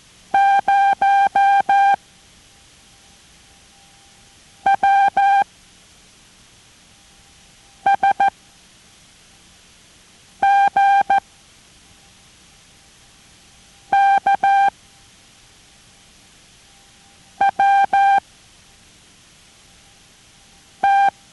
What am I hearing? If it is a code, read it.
Morse code, heard as a keyed tone, about 14 wpm, spaced as for 3 wpm, 790 Hz. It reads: 0WSGKWT